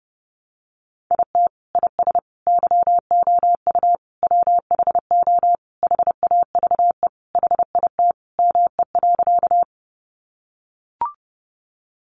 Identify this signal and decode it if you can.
Morse code, heard as a keyed tone, about 30 wpm, 700 Hz. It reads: IT IS YOU WHO HAVE HIT ME.
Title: Sentences from Top 1200 Words 30wpm